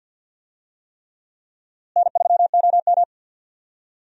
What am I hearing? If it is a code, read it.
Morse code, heard as a keyed tone, about 51 wpm, 700 Hz. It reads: N3OM
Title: USA Call Signs - 4 Characters - Repeated 3x - 50wpm